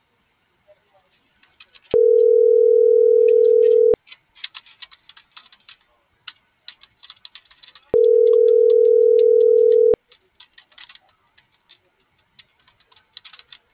Ringback tone (North America), 440 + 480 Hz.